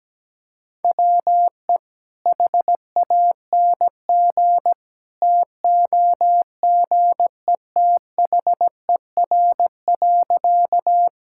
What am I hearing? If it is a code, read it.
Morse code, heard as a keyed tone, about 17 wpm, 700 Hz. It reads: WE HANG TOGETHER.
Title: Sentences from Top 1600 Words 17wpm